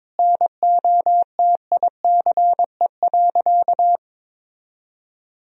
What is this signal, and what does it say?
Morse code, heard as a keyed tone, about 22 wpm, 700 Hz. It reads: NOTICE.
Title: Sentences from Top 1000 Words - Review Part 2 - 22wpm